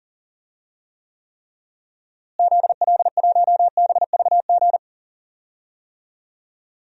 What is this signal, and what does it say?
Morse code, heard as a keyed tone, about 40 wpm, 700 Hz. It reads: ZL1BVG
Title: DXCC-100 Call Signs - 6 Characters - Repeated 3x - 40wpm